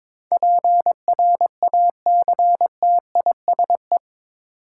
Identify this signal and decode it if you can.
Morse code, heard as a keyed tone, about 21 wpm, 700 Hz. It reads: PRACTISE
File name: practice / 22wpm / Eight Letter Words - 22wpm